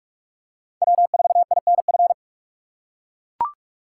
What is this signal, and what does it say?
Morse code, heard as a keyed tone, about 41 wpm, 700 Hz. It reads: W4INF